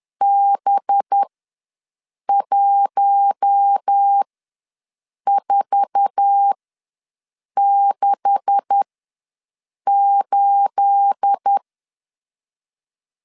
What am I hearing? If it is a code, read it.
Morse code, heard as a keyed tone, about 10 wpm, spaced as for 6 wpm, 800 Hz. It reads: B1468
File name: LEZIONI / 9 286 ws 10wpm 4032